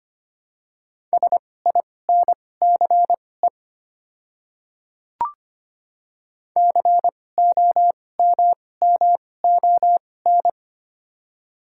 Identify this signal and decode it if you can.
Morse code, heard as a keyed tone, about 25 wpm, spaced as for 18 wpm, 700 Hz. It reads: SINCE COMMON
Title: 2x Character Spacing - Top 500 Words - 25wpm